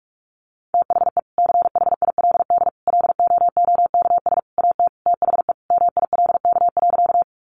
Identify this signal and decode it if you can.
Morse code, heard as a keyed tone, about 44 wpm, 700 Hz. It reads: THE CHILD LOOKS AT THE MILK.
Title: Additional Easy Phrases 45wpm